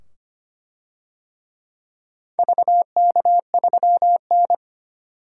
Morse code, heard as a keyed tone, about 25 wpm, 700 Hz. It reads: VK3N
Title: DXCC-100 Call Signs - 4 Characters - 25wpm